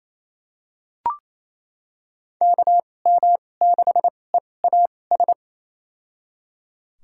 Morse code, heard as a keyed tone, about 28 wpm, spaced as for 20 wpm, 700 Hz. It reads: KM6EAS
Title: USA Call Signs - 6 Characters - 2x Character Spacing - 28wpm